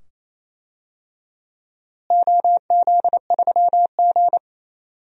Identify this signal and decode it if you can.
Morse code, heard as a keyed tone, about 28 wpm, 700 Hz. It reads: OZ3G
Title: DXCC-100 Call Signs - 4 Characters - Repeated 2x - 28wpm